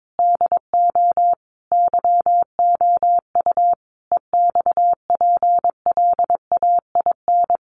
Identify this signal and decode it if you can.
Morse code, heard as a keyed tone, about 22 wpm, 700 Hz. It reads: DO YOU EXPLAIN
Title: Sentences from Top 1100 Words 22wpm